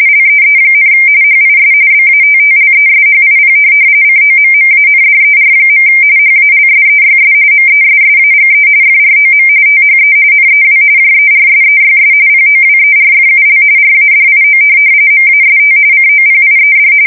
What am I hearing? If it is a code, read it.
RTTY, about 45.45 baud. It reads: LEE, HIS BATTLESHIP COMMANDER. HALSEYS MESSAGE FROM THE NEW JERSEYS POWERFUL RADIO MAST COPIED
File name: fsk170 / USS-New-Jersey